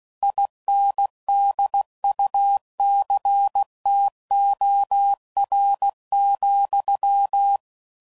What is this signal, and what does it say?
Morse code, heard as a keyed tone, about 16 wpm, 800 Hz. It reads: INDUCTOR,
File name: CW Code Groups @15WPM / Technical Subjects / Reactance